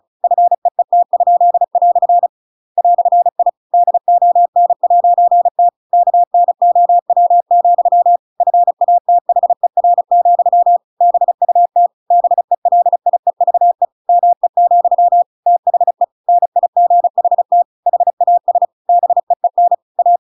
Morse code, heard as a keyed tone, about 34 wpm, 700 Hz. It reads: FEET?" "I DON'T KNOW, FATHER, BUT BELIEVE ME, THE NIGHT HAS BEEN A
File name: Pinocchio / Pinocchio ch7 35 35 ik1hge